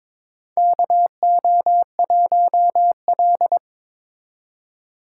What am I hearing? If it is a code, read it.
Morse code, heard as a keyed tone, about 22 wpm, 700 Hz. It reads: KO1L
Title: USA Call Signs - 4 Characters - 2x Character Spacing - 22wpm